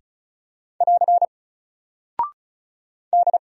Morse code, heard as a keyed tone, about 36 wpm, 700 Hz. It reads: <AR> D
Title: Single Character (Letter:Number:Punctuation:Prosign) 35wpm